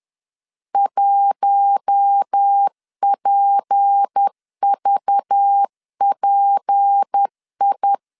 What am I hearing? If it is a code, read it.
Morse code, heard as a keyed tone, about 10 wpm, 800 Hz. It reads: 1PVPI
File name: LEZIONI / 6 PKX0 / 6 P 10wpm 2432